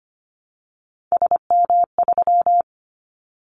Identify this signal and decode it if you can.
Morse code, heard as a keyed tone, about 25 wpm, 700 Hz. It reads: SM3
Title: Callsign Prefixes for Top 25 DXCC - Repeated 3x - 25wpm